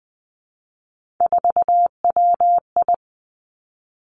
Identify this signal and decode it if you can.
Morse code, heard as a keyed tone, about 20 wpm, 700 Hz. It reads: 4WI
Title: Three Challenging Letter-Number Combinations 20wpm